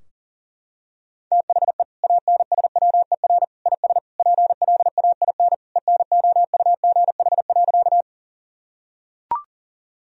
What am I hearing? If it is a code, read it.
Morse code, heard as a keyed tone, about 37 wpm, 700 Hz. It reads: THE ANSWER IS PLAIN ENOUGH.
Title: Sentences from Top 1000 Words - Review Part 2 - 40wpm